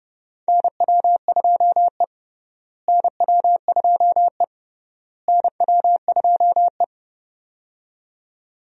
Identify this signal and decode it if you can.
Morse code, heard as a keyed tone, about 30 wpm, 700 Hz. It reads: NW2E NW2E NW2E
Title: USA Call Signs - 4 Characters - Repeated 3x - 30wpm